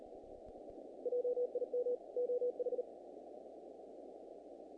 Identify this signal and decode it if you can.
Morse code, heard as a keyed tone, about 36 wpm, 490 Hz. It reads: JIM OH
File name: contests / naqp-cw-k5tr-op-k5pi- -1900